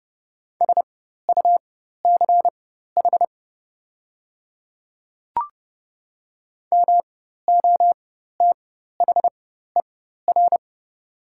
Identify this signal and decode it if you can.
Morse code, heard as a keyed tone, about 30 wpm, spaced as for 14 wpm, 700 Hz. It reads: SUCH MOTHER
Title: 4x Character Spacing - Top 500 Words - 30wpm